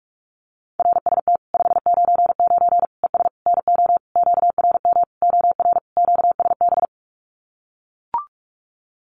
Morse code, heard as a keyed tone, about 44 wpm, 700 Hz. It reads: RST 599 ES NO QRM OR QSB
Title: Common QSO Phrases 45wpm